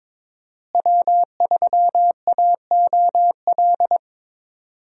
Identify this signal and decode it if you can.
Morse code, heard as a keyed tone, about 21 wpm, 700 Hz. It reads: W3AOL
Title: USA Call Signs - 5 Characters - Repeated 2x - 22wpm